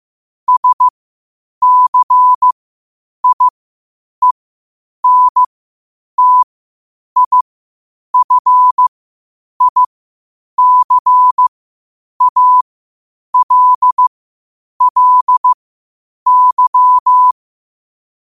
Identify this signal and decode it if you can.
Morse code, heard as a keyed tone, about 15 wpm, spaced as for 8 wpm, 1000 Hz. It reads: SCIENTIFICALLY
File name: qotdfiles / 8wpm